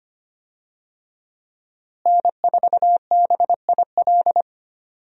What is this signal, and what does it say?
Morse code, heard as a keyed tone, about 25 wpm, 700 Hz. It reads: N4BIL
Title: USA Call Signs - 5 Characters - Repeated 2x - 25wpm